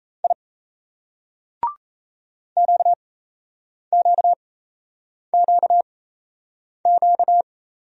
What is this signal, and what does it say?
Morse code, heard as a keyed tone, about 37 wpm, spaced as for 8 wpm, 700 Hz. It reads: I QQQQ